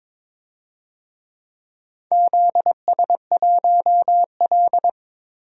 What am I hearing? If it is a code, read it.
Morse code, heard as a keyed tone, about 23 wpm, 700 Hz. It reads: ZS1L
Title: DXCC-100 Call Signs - 4 Characters - 22wpm